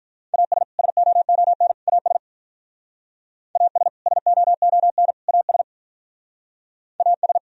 Morse code, heard as a keyed tone, about 47 wpm, 700 Hz. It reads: AS SOON AS AS SOON AS AS